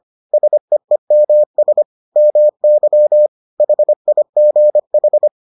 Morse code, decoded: SEEMS MY HIGH

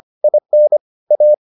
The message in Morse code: IN A

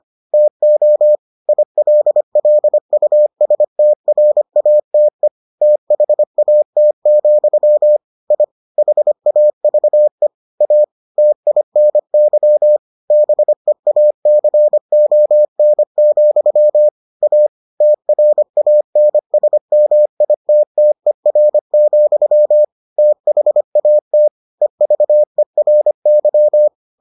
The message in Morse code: TO ILLUSTRATE THAT, I HAVE A TINY BEACON, A TRANSMITTER, THAT EVERY

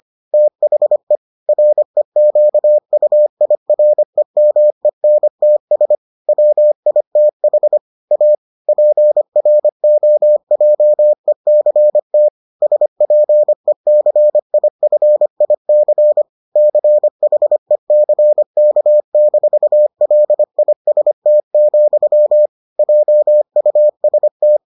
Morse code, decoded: THE REQUIREMENTS WITH A PROJECT SPECIFIC CHECK-LIST, JUST

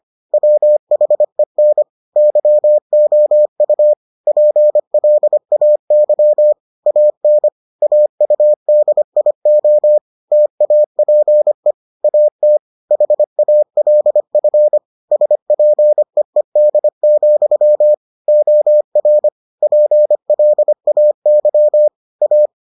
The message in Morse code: WHEN YOU PLAY AN AUDIO TAPE AT HALF SPEED, OR PLAY A

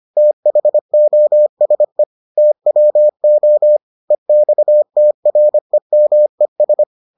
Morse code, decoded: THOSE TWO EXTREMES

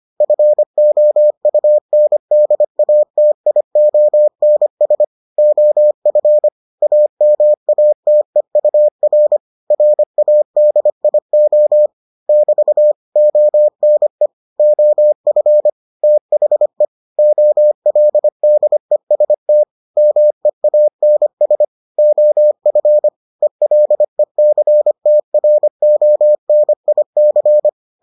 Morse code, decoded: FOUNDATIONS OF AMATEUR RADIO <BT> ONE OF THE OLDEST MEANS OF ELECTRONIC

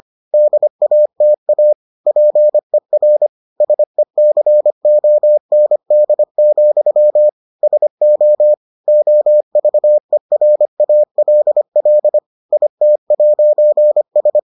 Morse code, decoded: DATA PER SECOND, SO OVERALL IT'S

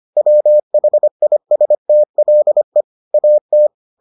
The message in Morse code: WHISTLE AT

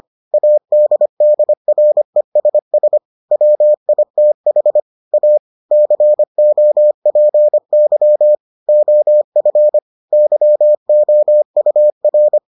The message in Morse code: ADDRESS WITH A COPY OF YOUR

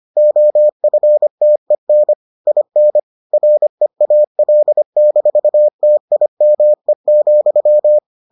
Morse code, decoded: OFTEN IN REAL-TIME,

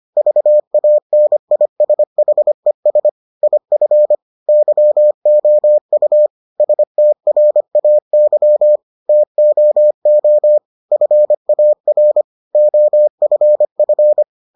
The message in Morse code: VANISHES IF YOU STRAY TOO FAR OFF

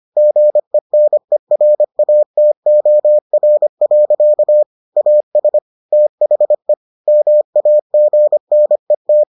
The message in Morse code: GENERATOR. AS THE MAGNET

